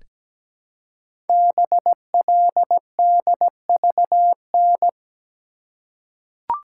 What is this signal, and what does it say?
Morse code, heard as a keyed tone, about 17 wpm, 700 Hz. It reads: BLDVN
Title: Code Groups - 5 Letter:Numbers - 17wpm